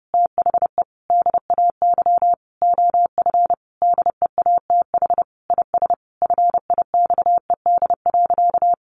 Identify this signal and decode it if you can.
Morse code, heard as a keyed tone, about 28 wpm, 700 Hz. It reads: THE DAY OF DEATH IS FIXED.
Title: Sentences from Top 1100 Words 30wpm